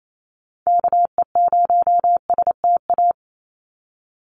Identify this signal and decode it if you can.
Morse code, heard as a keyed tone, about 28 wpm, 700 Hz. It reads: KE0STA
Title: USA Call Signs 28wpm